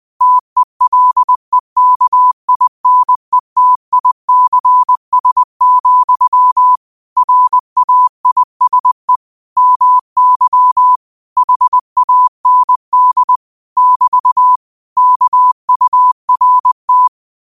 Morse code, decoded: TELEKINETICS, RAISE MY HAND <BT> KURT

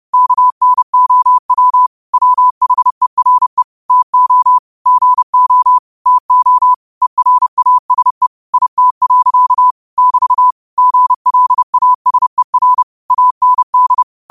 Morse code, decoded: KNOW WHERE TO GO TO ERASE IT. <BT> GLASER AND

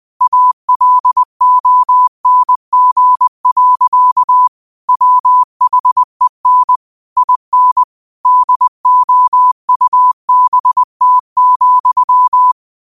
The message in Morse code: ALONG. WHEN IN DOUBT,

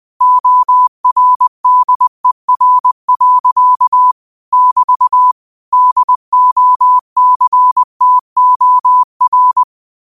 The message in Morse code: ORDER. <BT> DOCTOR

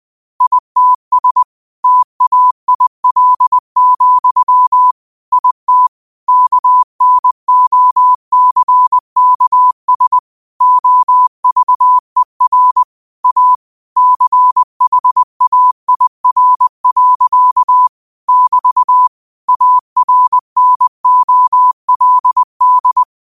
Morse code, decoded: ITS TAIL, IT KNOCKS OVER A CHAIR. <BT> ARNOLD

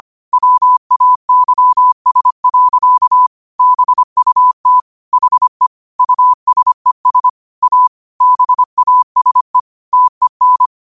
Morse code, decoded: WAYS. BUT HE USES A BASE TEN

